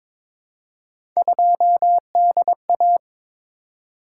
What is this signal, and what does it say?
Morse code, heard as a keyed tone, about 22 wpm, 700 Hz. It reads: 2DA
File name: practice / 22wpm / Three Letter-Number Combinations - 22wpm